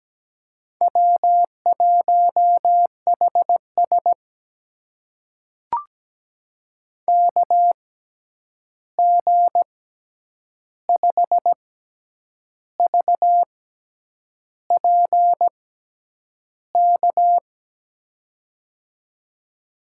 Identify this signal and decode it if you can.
Morse code, heard as a keyed tone, about 17 wpm, spaced as for 8 wpm, 700 Hz. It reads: W1HS KG5VPK
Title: USA Call Signs - 6x Character Spacing - 17wpm